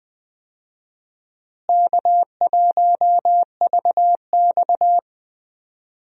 Morse code, heard as a keyed tone, about 20 wpm, 700 Hz. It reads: K1VX
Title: USA Call Signs - 4 Characters - Repeated 3x - 20wpm